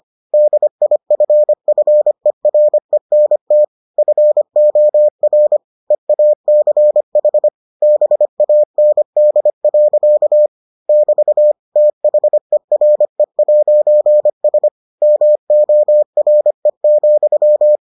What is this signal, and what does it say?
Morse code, heard as a keyed tone, about 25 wpm, 600 Hz. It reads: DIFFERENT FOR EACH BAND. <BT> THERE'S MORE,